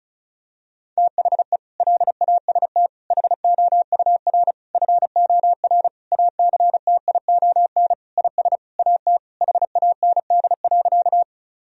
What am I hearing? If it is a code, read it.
Morse code, heard as a keyed tone, about 36 wpm, 700 Hz. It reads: THE LAST HOUR FOR ACTION IS AT HAND.
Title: Sentences from Top 1000 Words - Review Part 4 - 35wpm